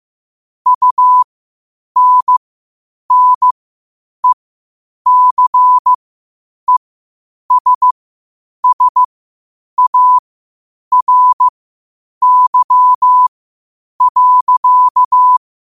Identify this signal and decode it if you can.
Morse code, heard as a keyed tone, about 15 wpm, spaced as for 8 wpm, 1000 Hz. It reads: UNNECESSARY.
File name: qotdfiles / 8wpm